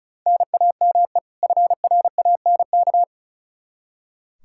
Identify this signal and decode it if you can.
Morse code, heard as a keyed tone, about 36 wpm, 700 Hz. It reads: NAME FRANK
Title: Practice QSO - Exchange 1 - Beginning - 35wpm